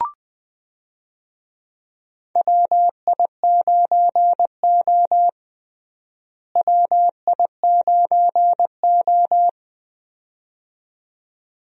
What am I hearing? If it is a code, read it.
Morse code, heard as a keyed tone, about 20 wpm, 700 Hz. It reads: WI9O WI9O
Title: USA Call Signs - 4 Characters - Repeated 2x - 20wpm